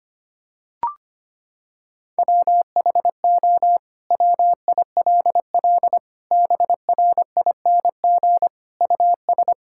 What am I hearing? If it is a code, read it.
Morse code, heard as a keyed tone, about 25 wpm, 700 Hz. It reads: WHO WILL BRING US
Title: Sentences from Top 1600 Words 25wpm